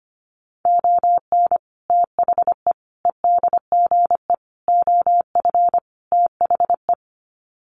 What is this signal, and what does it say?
Morse code, heard as a keyed tone, about 25 wpm, 700 Hz. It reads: ON THE EDGE OF THE
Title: Sets of 5 Words 25wpm